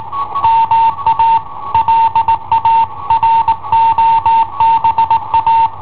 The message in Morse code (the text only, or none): MA LA ROBA